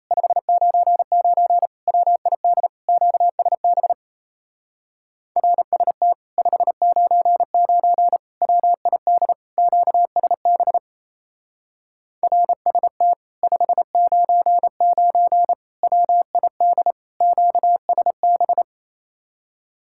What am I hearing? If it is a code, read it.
Morse code, heard as a keyed tone, about 34 wpm, 700 Hz. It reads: 599 WID QSB RST 599 WID QSB RST 599 WID QSB
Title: Speed-Racing Common QSO Phrases - 28wpm